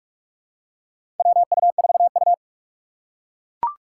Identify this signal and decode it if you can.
Morse code, heard as a keyed tone, about 44 wpm, 690 Hz. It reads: WU4U
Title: USA Call Signs - 4 Characters - 3x Character Spacing - 45wpm